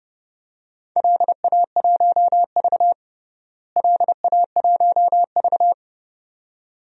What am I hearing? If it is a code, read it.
Morse code, heard as a keyed tone, about 30 wpm, 700 Hz. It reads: LA1V LA1V
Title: DXCC-100 Call Signs - 4 Characters - Repeated 2x - 30wpm